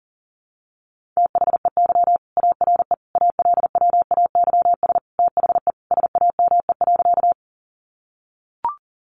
Morse code, heard as a keyed tone, about 40 wpm, 700 Hz. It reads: THEY ARE ALWAYS THE SAME.
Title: Sentences from Top 300 Words 40wpm